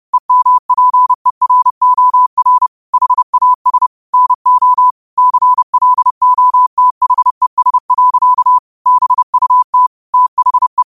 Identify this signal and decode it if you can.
Morse code, heard as a keyed tone, about 30 wpm, 1000 Hz. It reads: EMPEROR HAS NO CLOTHES. BUT THE
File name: qotdfiles / 30wpm